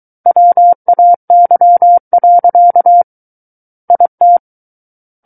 Morse code, heard as a keyed tone, about 23 wpm, 700 Hz. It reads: WAY. IT
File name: Book One / 14-Chapter 14 - 23 WPM